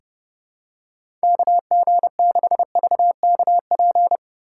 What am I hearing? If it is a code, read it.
Morse code, heard as a keyed tone, about 32 wpm, 700 Hz. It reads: KG6VKP
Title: USA Call Signs - 6 Characters - 4x Character Spacing - 30wpm